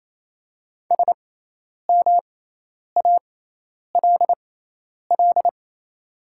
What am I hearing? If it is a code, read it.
Morse code, heard as a keyed tone, about 27 wpm, spaced as for 10 wpm, 700 Hz. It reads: SMALL
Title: 6x Character Spacing - Top 200 Words - 28wpm